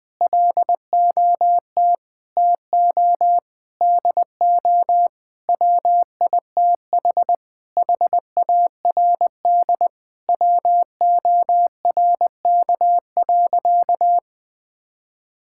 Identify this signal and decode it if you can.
Morse code, heard as a keyed tone, about 20 wpm, 700 Hz. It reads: LOT TO DO WITH HARD WORK.
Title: Sentences with 6-Word Phrases - Spoken Beginning - 20wpm